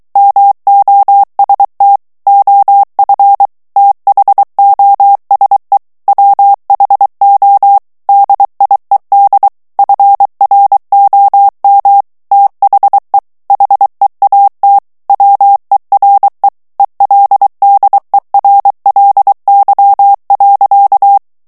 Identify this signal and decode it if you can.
Morse code, heard as a keyed tone, about 23 wpm, 800 Hz. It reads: MOST OF THOSE WHO DIED FROM THE HEAT WERE ELDERLY.
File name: morse mp3 / ANGLAIS texte clair 2 vitesse 22 mots minute